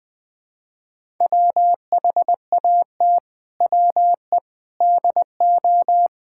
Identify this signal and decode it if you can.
Morse code, heard as a keyed tone, about 20 wpm, 700 Hz. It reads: WHAT WE DO
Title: Sentences with 7-Word Phrases - 20wpm